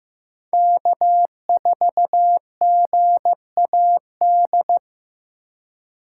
Morse code, heard as a keyed tone, about 15 wpm, 700 Hz. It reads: K4GAD
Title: USA Call Signs - 5 Characters - 2x Character Spacing - 15wpm